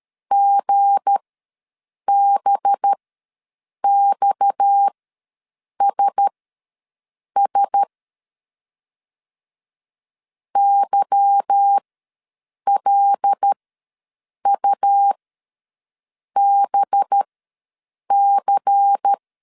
Morse code, heard as a keyed tone, about 13 wpm, spaced as for 7 wpm, 800 Hz. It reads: GBXSS YLUBC
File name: LEZIONI / 8 YJ45 / 8 YJ ws 12wpm 3342